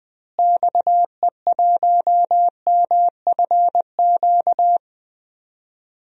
Morse code, heard as a keyed tone, about 20 wpm, 700 Hz. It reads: XE1MFQ
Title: DXCC-100 Call Signs - 6 Characters - Repeated 2x - 20wpm